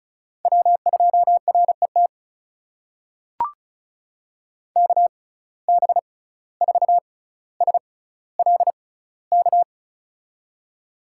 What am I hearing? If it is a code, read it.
Morse code, heard as a keyed tone, about 34 wpm, spaced as for 17 wpm, 700 Hz. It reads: W2RET KB4SLK
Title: USA Call Signs - 6x Character Spacing - 35wpm